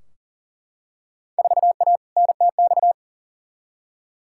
Morse code, heard as a keyed tone, about 40 wpm, 700 Hz. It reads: 4A NTX
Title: Field Day Exchanges - 40wpm